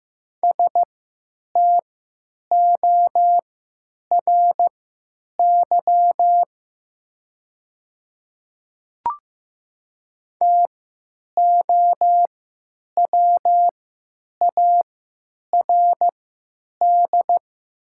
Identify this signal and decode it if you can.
Morse code, heard as a keyed tone, about 15 wpm, spaced as for 9 wpm, 700 Hz. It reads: STORY TOWARD
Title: 3x Character Spacing - Top 500 Words - 15wpm